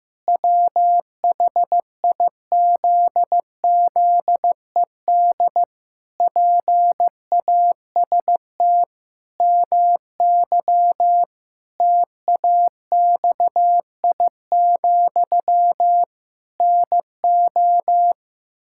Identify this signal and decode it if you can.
Morse code, heard as a keyed tone, about 16 wpm, 700 Hz. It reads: WHIZZED PAST MY TAXI, NO
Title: Pangram Challenge v2 - 15wpm